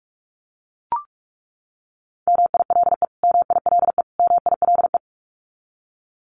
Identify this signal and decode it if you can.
Morse code, heard as a keyed tone, about 44 wpm, 700 Hz. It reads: MILE MILE MILE
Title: Top 500 Words Repeated 45wpm